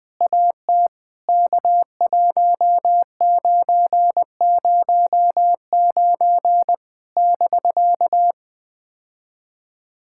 Morse code, heard as a keyed tone, about 20 wpm, 700 Hz. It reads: AT K1909 <BK>
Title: POTA QSO Practice - 20wpm